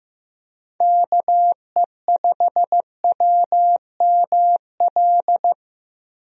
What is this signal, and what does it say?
Morse code, heard as a keyed tone, about 15 wpm, 700 Hz. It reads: KE5WML
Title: USA Call Signs - 6 Characters - Repeated 2x - 15wpm